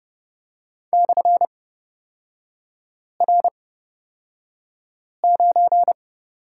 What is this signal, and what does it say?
Morse code, heard as a keyed tone, about 29 wpm, spaced as for 5 wpm, 690 Hz. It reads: /R9
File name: Lesson 37e - Letter x - Three Characters 30wpm